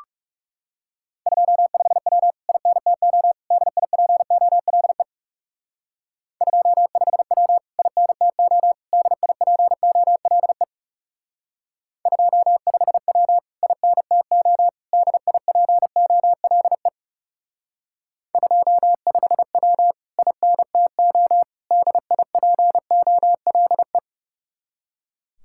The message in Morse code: 25W INTO DIPOLE 25W INTO DIPOLE 25W INTO DIPOLE 25W INTO DIPOLE

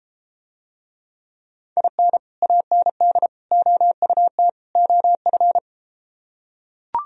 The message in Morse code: IN AND OUT OF

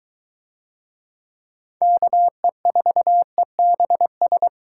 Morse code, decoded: KE4EBS

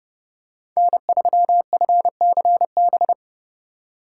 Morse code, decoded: N3FCB

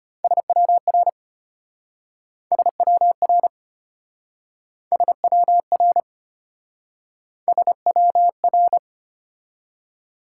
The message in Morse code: SWR SWR SWR SWR

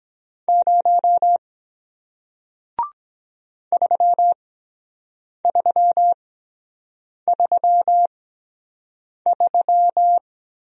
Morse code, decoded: 0 3333

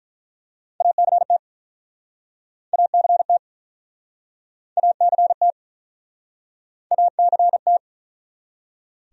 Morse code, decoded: ACT ACT ACT ACT

